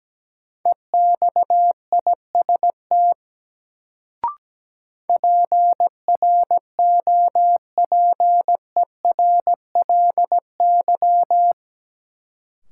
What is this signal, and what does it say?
Morse code, heard as a keyed tone, about 17 wpm, 700 Hz. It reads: EXIST PROPERLY
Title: Sentences from Top 1100 Words - New Word Drill Alt - 17wpm